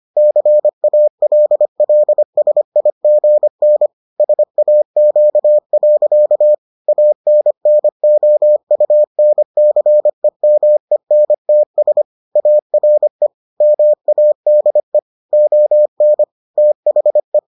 Morse code, heard as a keyed tone, about 25 wpm, 600 Hz. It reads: CALLSIGN SAQ. ANNOUNCEMENTS ARE MADE ON THE